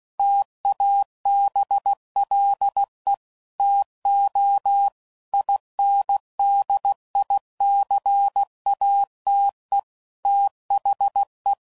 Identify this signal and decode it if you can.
Morse code, heard as a keyed tone, about 16 wpm, 800 Hz. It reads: TABLE TO INDICATE THE
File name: CW Code Groups @15WPM / Technical Subjects / Gates Intro